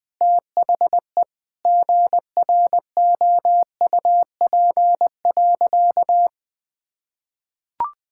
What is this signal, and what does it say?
Morse code, heard as a keyed tone, about 20 wpm, 700 Hz. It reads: THE GROUP.
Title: Sentences with 5-Word Phrases - 20wpm